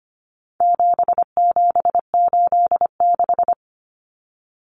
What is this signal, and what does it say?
Morse code, heard as a keyed tone, about 25 wpm, 700 Hz. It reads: 7786
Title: Numbers - 4-Digit 25wpm